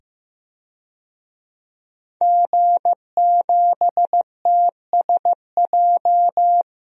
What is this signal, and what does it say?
Morse code, heard as a keyed tone, about 15 wpm, 700 Hz. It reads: G7TSJ
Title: DXCC-100 Call Signs - 5 Characters - 15wpm